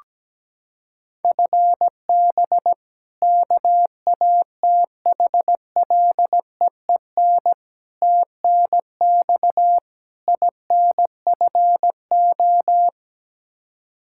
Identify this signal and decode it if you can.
Morse code, heard as a keyed tone, about 17 wpm, 700 Hz. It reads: FB KATHLEEN TNX INFO
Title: Practice QSO - Exchange 3 - More Info - 17wpm